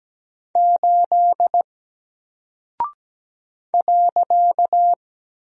Morse code, heard as a keyed tone, about 17 wpm, 700 Hz. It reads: 8 .
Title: Single Character (Letter:Number:Punctuation:Prosign) 17wpm